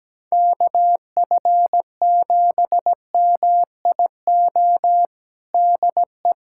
Morse code, heard as a keyed tone, about 17 wpm, 700 Hz. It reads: KF7MIO DE
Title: USA Callsign DE Callsign 17wpm